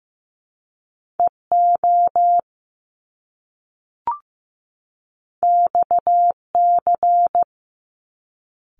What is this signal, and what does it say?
Morse code, heard as a keyed tone, about 15 wpm, 700 Hz. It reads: EO XC
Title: Top Two Letter Combinations in Words 15wpm